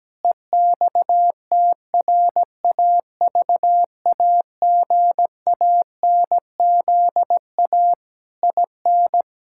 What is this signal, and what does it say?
Morse code, heard as a keyed tone, about 17 wpm, 700 Hz. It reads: EXTRAVAGANZA IN
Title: Pangram Challenge v2 - 17wpm